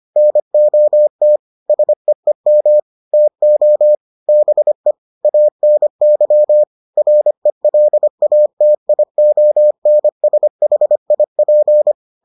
Morse code, heard as a keyed tone, about 25 wpm, 600 Hz. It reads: NOT SEEM TO BE ANY RELATIONSHIP